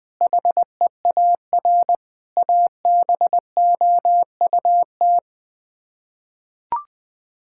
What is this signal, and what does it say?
Morse code, heard as a keyed tone, about 20 wpm, 700 Hz. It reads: HEAR ABOUT
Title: Sets of 6 Words - 20wpm